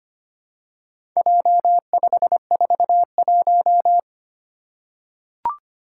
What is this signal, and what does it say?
Morse code, heard as a keyed tone, about 25 wpm, 700 Hz. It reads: J541
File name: practice / 25wpm / Four Letter-Number Combinations - 25wpm